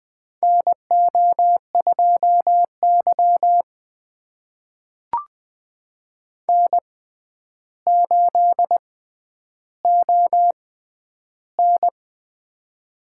Morse code, heard as a keyed tone, about 20 wpm, spaced as for 10 wpm, 700 Hz. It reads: NO2Y N8ON
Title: USA Call Signs - 6x Character Spacing - 20wpm